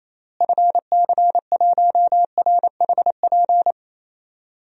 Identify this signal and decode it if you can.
Morse code, heard as a keyed tone, about 28 wpm, 700 Hz. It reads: FC1RHP
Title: DXCC-100 Call Signs - 6 Characters - Repeated 3x - 28wpm